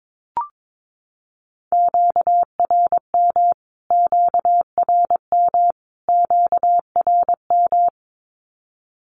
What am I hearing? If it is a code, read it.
Morse code, heard as a keyed tone, about 22 wpm, 700 Hz. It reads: QRM QRM QRM
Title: Q-Codes Repeated 3x 22wpm